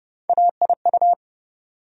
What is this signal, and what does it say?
Morse code, heard as a keyed tone, about 29 wpm, 700 Hz. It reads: AIU
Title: USA Call Sign Postfixes - 3 Letters 30wpm